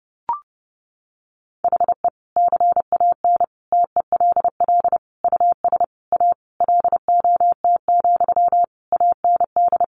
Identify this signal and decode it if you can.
Morse code, heard as a keyed tone, about 30 wpm, 700 Hz. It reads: HE CAN TELL US A LOT, AND
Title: Sentences from Top 1300 Words 30wpm